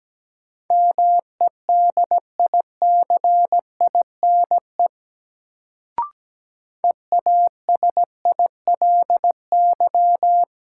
Morse code, heard as a keyed tone, about 17 wpm, 700 Hz. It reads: MEDICINE EASILY
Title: Sentences from Top 900 Words - New Word Drill - 17wpm